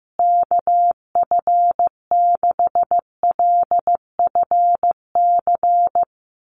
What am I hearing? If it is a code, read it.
Morse code, heard as a keyed tone, about 15 wpm, 700 Hz. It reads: KF6LFC
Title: USA Call Signs Repeated 3x 15wpm